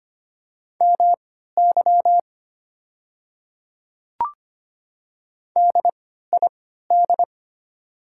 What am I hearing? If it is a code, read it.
Morse code, heard as a keyed tone, about 25 wpm, spaced as for 14 wpm, 700 Hz. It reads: MY DID
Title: 3x Character Spacing - Top 100 Words - 25wpm